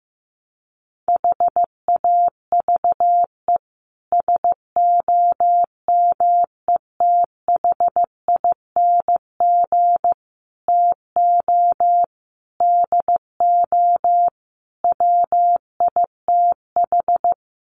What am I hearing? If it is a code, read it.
Morse code, heard as a keyed tone, about 15 wpm, 700 Hz. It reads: HAVE SOMETHING TO DO WITH
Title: Sets of 5 Words Encore 15wpm